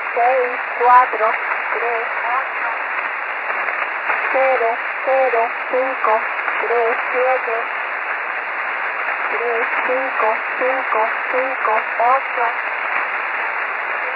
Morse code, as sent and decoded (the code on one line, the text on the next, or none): none